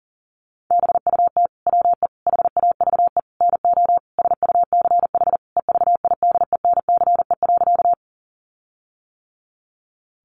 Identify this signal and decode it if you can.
Morse code, heard as a keyed tone, about 40 wpm, 700 Hz. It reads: BUT WE HAVE NO SUCH EVIDENCE.
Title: Sentences from Top 1300 Words 40wpm